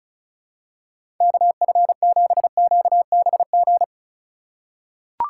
Morse code, heard as a keyed tone, about 36 wpm, 700 Hz. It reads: KF7QBG
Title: USA Call Signs - 6 Characters - 2x Character Spacing - 35wpm